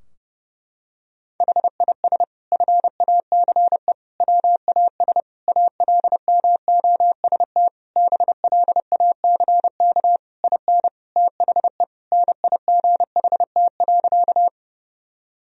Morse code, heard as a keyed tone, about 30 wpm, 700 Hz. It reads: HIS FACE WAS ALMOST BLACK IN THE NIGHT.
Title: Sentences from Top 1000 Words - Review Part 4 - 30wpm